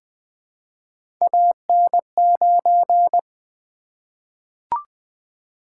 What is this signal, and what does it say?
Morse code, heard as a keyed tone, about 20 wpm, 700 Hz. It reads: AN9
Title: Three Letter-Number Combinations - 20wpm